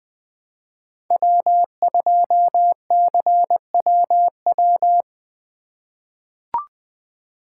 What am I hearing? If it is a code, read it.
Morse code, heard as a keyed tone, about 20 wpm, 700 Hz. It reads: W2CWW
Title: USA Call Signs - 5 Characters - 4x Character Spacing - 20wpm